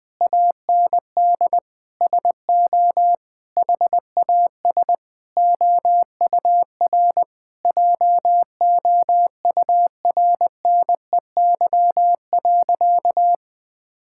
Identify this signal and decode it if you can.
Morse code, heard as a keyed tone, about 20 wpm, 700 Hz. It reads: AND SO HAS OUR JOURNEY.
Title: Sentences with 9-Word Phrases - 20wpm